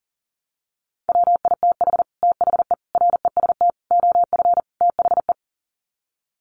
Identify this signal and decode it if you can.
Morse code, heard as a keyed tone, about 40 wpm, 700 Hz. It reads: WITH THE REST OF THE
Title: Sets of 5 Words 40wpm